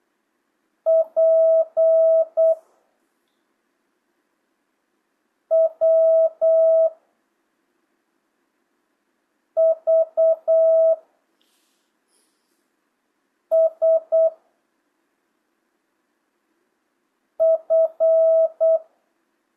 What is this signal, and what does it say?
Morse code, heard as a keyed tone, about 8 wpm, 650 Hz. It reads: P W V S F
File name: jh8cbh / 2025cw